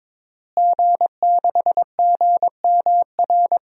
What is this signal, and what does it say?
Morse code, heard as a keyed tone, about 22 wpm, 700 Hz. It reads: G6GMR
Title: DXCC-100 Call Signs - 5 Characters - Repeated 2x - 22wpm